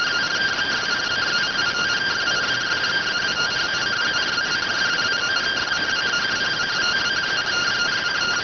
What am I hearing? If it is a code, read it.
Two-tone teleprinter signal, about 170 Hz shift.